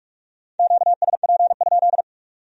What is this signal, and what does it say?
Morse code, heard as a keyed tone, about 55 wpm, 700 Hz. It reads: QSP?
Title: Speed-Racing Q-Codes - 45wpm